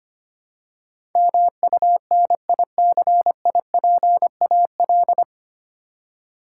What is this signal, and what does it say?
Morse code, heard as a keyed tone, about 25 wpm, 700 Hz. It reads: MUNICIPAL
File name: Nine Letter Words - 25wpm